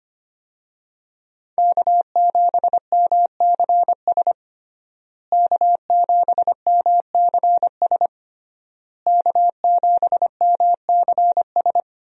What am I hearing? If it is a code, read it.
Morse code, heard as a keyed tone, about 25 wpm, 700 Hz. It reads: K7MCS K7MCS K7MCS
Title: USA Call Signs - 5 Characters - Repeated 3x - 25wpm